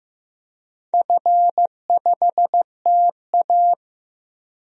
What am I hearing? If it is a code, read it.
Morse code, heard as a keyed tone, about 15 wpm, 700 Hz. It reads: F5TA
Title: DXCC-100 Call Signs - 4 Characters - Repeated 2x - 15wpm